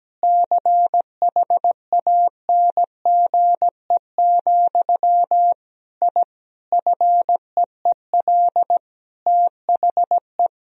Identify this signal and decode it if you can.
Morse code, heard as a keyed tone, about 17 wpm, 700 Hz. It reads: CHANGE, I FEEL THE
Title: Sentences from Top 1000 Words - Review Part 1 - 17wpm